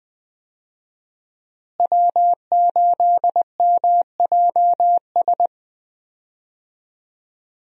Morse code, heard as a keyed tone, about 20 wpm, 700 Hz. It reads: W8MJS
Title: USA Call Signs - 5 Characters - Repeated 2x - 20wpm